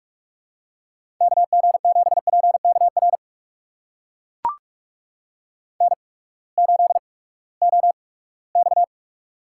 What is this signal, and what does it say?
Morse code, heard as a keyed tone, about 44 wpm, spaced as for 22 wpm, 700 Hz. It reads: KG7PKR N8OX
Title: USA Call Signs - 8x Character Spacing - 45wpm